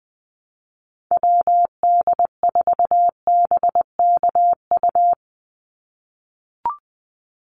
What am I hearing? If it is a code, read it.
Morse code, heard as a keyed tone, about 20 wpm, 700 Hz. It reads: WD4BKU
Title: USA Call Signs 20wpm